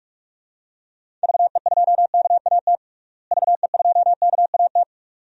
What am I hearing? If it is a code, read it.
Morse code, heard as a keyed tone, about 41 wpm, 700 Hz. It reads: VE2KAT VE2KAT
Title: DXCC-100 Call Signs - 6 Characters - Repeated 2x - 45wpm